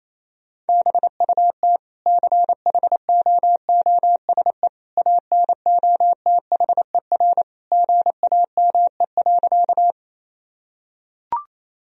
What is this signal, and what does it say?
Morse code, heard as a keyed tone, about 28 wpm, 700 Hz. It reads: BUT CHOOSE ANOTHER GAME.
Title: Sentences from Top 1000 Words - Review Part 4 - 28wpm